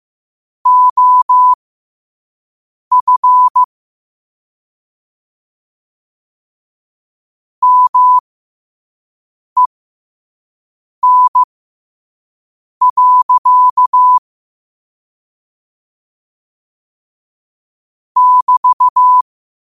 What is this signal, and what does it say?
Morse code, heard as a keyed tone, about 15 wpm, spaced as for 5 wpm, 1000 Hz. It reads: OF MEN. <BT>
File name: qotdfiles / 5wpm